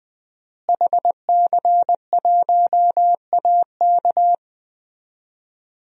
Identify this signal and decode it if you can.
Morse code, heard as a keyed tone, about 20 wpm, 700 Hz. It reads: HC1AK
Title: DXCC-100 Call Signs - 5 Characters - Repeated 2x - 20wpm